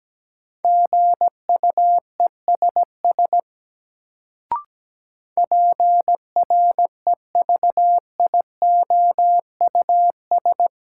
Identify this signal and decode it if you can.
Morse code, heard as a keyed tone, about 17 wpm, 700 Hz. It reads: GUESS PREVIOUS
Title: Sentences from Top 1500 Words - New Word Drill - 17wpm